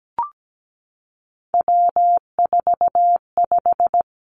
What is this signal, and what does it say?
Morse code, heard as a keyed tone, about 17 wpm, 700 Hz. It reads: W45
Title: Three Challenging Letter-Number Combinations 17wpm